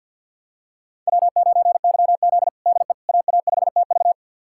Morse code, heard as a keyed tone, about 50 wpm, 700 Hz. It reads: W9YZ DE AA5TV
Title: USA Callsign DE Callsign 50wpm